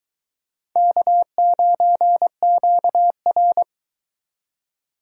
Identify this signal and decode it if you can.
Morse code, heard as a keyed tone, about 20 wpm, 700 Hz. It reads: K9QR